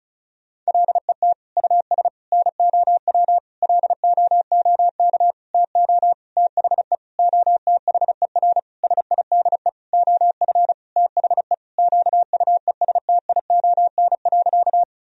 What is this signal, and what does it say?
Morse code, heard as a keyed tone, about 34 wpm, 700 Hz. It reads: LET US NOW LOOK TO THE OTHER SIDE OF THE QUESTION.
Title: Sentences from Top 1000 Words - Review Part 1 - 35wpm